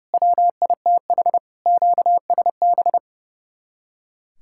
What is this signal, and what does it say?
Morse code, heard as a keyed tone, about 38 wpm, 700 Hz. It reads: WITH QSB